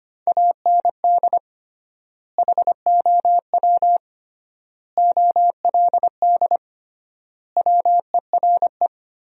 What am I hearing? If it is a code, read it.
Morse code, heard as a keyed tone, about 25 wpm, 700 Hz. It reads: AND HOW OLD WERE
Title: 3x Word-Spacing - Sentences from Top 200 Words - 25wpm